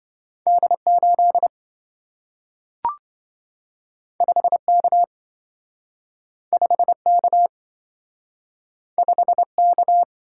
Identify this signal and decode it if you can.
Morse code, heard as a keyed tone, about 27 wpm, 700 Hz. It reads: D8 5K 5K 5K